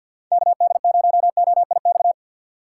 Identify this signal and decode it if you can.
Morse code, heard as a keyed tone, about 52 wpm, 700 Hz. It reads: KD0OIX
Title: K1USN SST QSO Practice - 50wpm